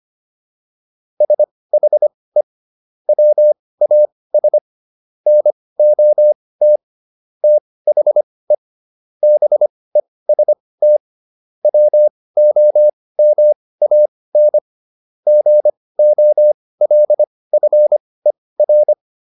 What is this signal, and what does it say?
Morse code, heard as a keyed tone, about 25 wpm, spaced as for 18 wpm, 600 Hz. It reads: SHE WAS NOT THE BEST WOMAN GOLFER